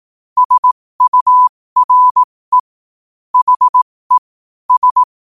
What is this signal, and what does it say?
Morse code, heard as a keyed tone, about 18 wpm, spaced as for 15 wpm, 1000 Hz. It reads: SURE HE S
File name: qotdfiles / 15wpm